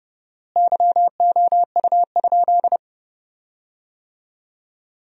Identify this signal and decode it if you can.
Morse code, heard as a keyed tone, about 30 wpm, 700 Hz. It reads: YOU?
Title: 4x Word-Spacing - Sentences from Top 100 Words - 30wpm